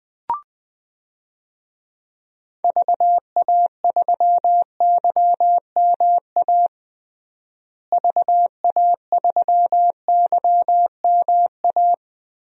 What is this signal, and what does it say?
Morse code, heard as a keyed tone, about 21 wpm, 700 Hz. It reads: VA3YMA VA3YMA
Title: DXCC-100 Call Signs - 6 Characters - Repeated 2x - 20wpm